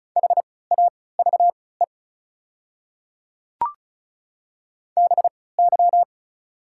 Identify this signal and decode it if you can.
Morse code, heard as a keyed tone, about 36 wpm, spaced as for 20 wpm, 700 Hz. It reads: HAVE BY
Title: 3x Character Spacing - Top 100 Words - 35wpm